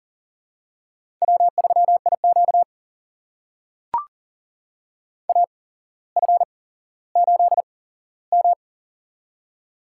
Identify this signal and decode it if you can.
Morse code, heard as a keyed tone, about 40 wpm, spaced as for 17 wpm, 700 Hz. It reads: W3IQ AF8M
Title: USA Call Signs - 8x Character Spacing - 40wpm